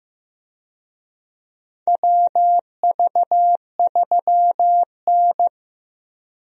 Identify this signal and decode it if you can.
Morse code, heard as a keyed tone, about 15 wpm, 700 Hz. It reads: WV3N
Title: USA Call Signs - 4 Characters - Repeated 3x - 15wpm